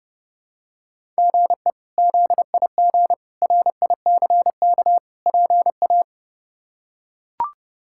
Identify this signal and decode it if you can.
Morse code, heard as a keyed tone, about 32 wpm, 700 Hz. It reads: GE ZIG RICK PA
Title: K1USN SST QSO Practice - 30wpm